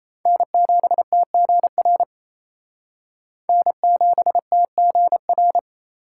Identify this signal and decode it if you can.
Morse code, heard as a keyed tone, about 30 wpm, 700 Hz. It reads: N7TGR N7TGR
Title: Speed Racing - USA 1x3 Call Signs 28wpm